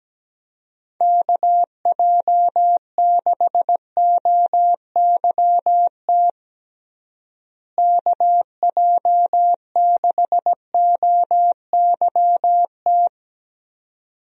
Morse code, heard as a keyed tone, about 17 wpm, 700 Hz. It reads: KJ6OYT KJ6OYT
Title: USA Call Signs - 6 Characters - Repeated 2x - 17wpm